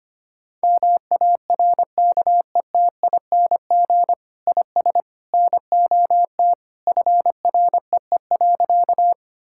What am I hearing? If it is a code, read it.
Morse code, heard as a keyed tone, about 25 wpm, 700 Hz. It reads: MARKETING IS NOT FREE.
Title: Sentences from Top 1000 Words - Review Part 5 - 25wpm